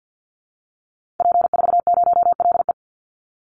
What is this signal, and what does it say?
Morse code, heard as a keyed tone, about 50 wpm, 700 Hz. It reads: P40LE
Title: Call Signs Repeated 2x - 50wpm